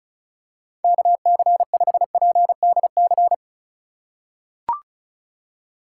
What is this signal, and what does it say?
Morse code, heard as a keyed tone, about 36 wpm, 700 Hz. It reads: KC5PDC
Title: USA Call Signs - 6 Characters - 4x Character Spacing - 35wpm